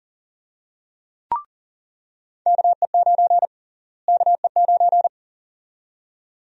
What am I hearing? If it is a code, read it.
Morse code, heard as a keyed tone, about 40 wpm, 700 Hz. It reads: KE9 KE9
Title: USA Call Sign Prefixes - Repeated 2x - 40wpm